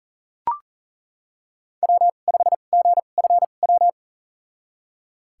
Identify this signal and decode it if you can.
Morse code, heard as a keyed tone, about 39 wpm, spaced as for 29 wpm, 700 Hz. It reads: W5GFW
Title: USA Call Signs - 2x Character Spacing - 40wpm